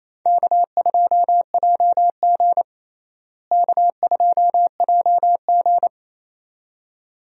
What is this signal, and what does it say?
Morse code, heard as a keyed tone, about 28 wpm, 700 Hz. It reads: K2JG K2JG
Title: USA Call Signs - 4 Characters - Repeated 2x - 28wpm